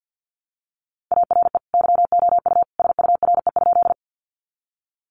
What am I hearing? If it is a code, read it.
Morse code, heard as a keyed tone, about 50 wpm, 700 Hz. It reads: ARE YOU SURE?
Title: Sentences from Top 300 Words 50wpm